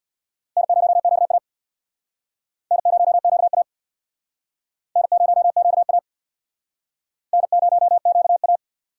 Morse code, heard as a keyed tone, about 68 wpm, 700 Hz. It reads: N0QA N0QA N0QA N0QA